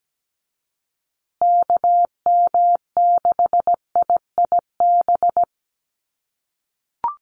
Morse code, heard as a keyed tone, about 17 wpm, 700 Hz. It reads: KM6IIB
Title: USA Call Signs Repeated 2x 17wpm